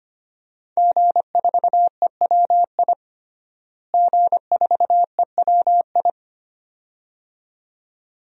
Morse code, heard as a keyed tone, about 25 wpm, 700 Hz. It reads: G4EWI G4EWI
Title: DXCC-100 Call Signs - 5 Characters - Repeated 2x - 25wpm